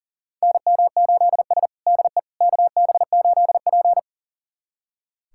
Morse code, heard as a keyed tone, about 37 wpm, 700 Hz. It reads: NM8S DE KB8P
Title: USA Callsign DE Callsign - 4 Characters - 40wpm